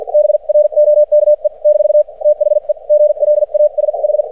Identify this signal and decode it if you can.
Morse code, heard as a keyed tone, about 53 wpm, spaced as for 44 wpm, 600 Hz. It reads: BROKE - THE GLASH
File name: archived / CRSnet